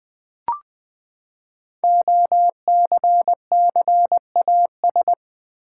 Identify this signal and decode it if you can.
Morse code, heard as a keyed tone, about 20 wpm, 700 Hz. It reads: OCCAS
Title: Build A Word - 5 Letters At A Time - 12-Letter Words - 20wpm